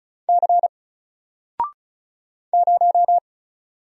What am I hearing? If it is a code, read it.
Morse code, heard as a keyed tone, about 34 wpm, 700 Hz. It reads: C 0
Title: Single Letter-Number 35wpm